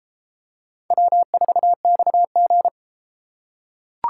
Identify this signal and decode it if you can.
Morse code, heard as a keyed tone, about 32 wpm, 700 Hz. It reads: W4XG